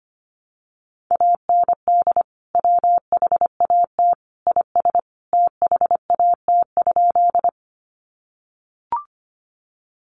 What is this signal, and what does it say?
Morse code, heard as a keyed tone, about 25 wpm, 700 Hz. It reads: AND WHAT IS THAT?
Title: Sentences from Top 100 Words 25wpm